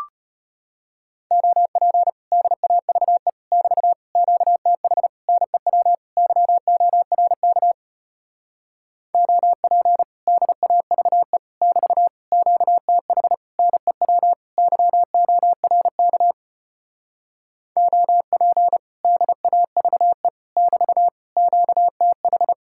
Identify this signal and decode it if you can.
Morse code, heard as a keyed tone, about 33 wpm, 700 Hz. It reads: OP DAVE <BT> QTH NEW YORK OP DAVE <BT> QTH NEW YORK OP DAVE <BT> QTH